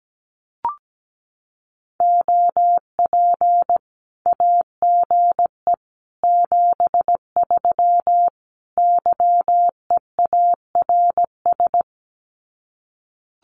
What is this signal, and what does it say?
Morse code, heard as a keyed tone, about 17 wpm, 700 Hz. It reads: OP AGE 73 YEARS
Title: Common QSO Phrases 17wpm